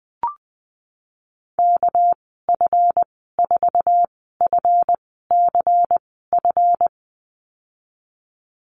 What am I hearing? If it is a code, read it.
Morse code, heard as a keyed tone, about 20 wpm, spaced as for 15 wpm, 700 Hz. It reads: KF4FCF
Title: 238 Lesson 38g - Letter q - Call Signs 20wpm